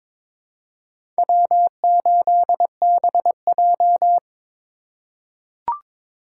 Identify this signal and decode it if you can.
Morse code, heard as a keyed tone, about 22 wpm, 700 Hz. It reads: W8BJ
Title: USA Call Signs - 4 Characters - 2x Character Spacing - 22wpm